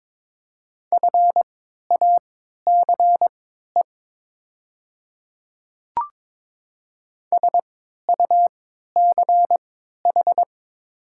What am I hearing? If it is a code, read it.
Morse code, heard as a keyed tone, about 22 wpm, spaced as for 12 wpm, 690 Hz. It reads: FACE SUCH
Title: 3x Character Spacing - Top 300 Words - 22wpm